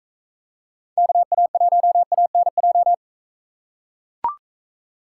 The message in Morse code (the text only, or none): KA1ANJ